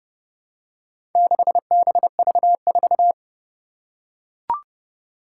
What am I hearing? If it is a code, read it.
Morse code, heard as a keyed tone, about 32 wpm, 700 Hz. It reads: 6BV4
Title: Four Challenging Letter-Number Combinations 30wpm